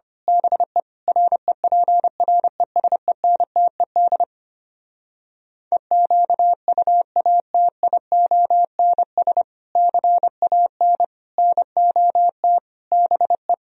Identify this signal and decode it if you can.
Morse code, heard as a keyed tone, about 25 wpm, 700 Hz. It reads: BE REPRESENTED EQUATIONS CAN NOT BE
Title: Speed-Racing - Sets of 5 Words 25wpm